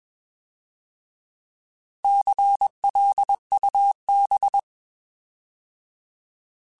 Morse code, heard as a keyed tone, about 21 wpm, 800 Hz. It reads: CLUB
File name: Alphabetical / Alphabetical mp3s / Cs @20